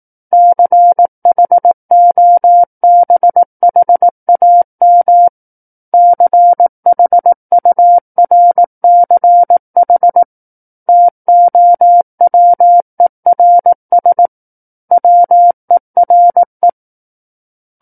Morse code, heard as a keyed tone, about 18 wpm, 700 Hz. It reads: CHOBHAM CHURCH TOWERS WERE
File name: 18 WPM